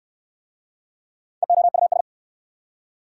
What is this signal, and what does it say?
Morse code, heard as a keyed tone, about 70 wpm, 700 Hz. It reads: EOUS